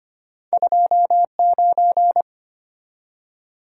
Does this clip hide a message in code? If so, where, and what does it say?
Yes: Morse code from 0.5 to 2.2 s, heard as a keyed tone, about 33 wpm, 700 Hz. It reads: 29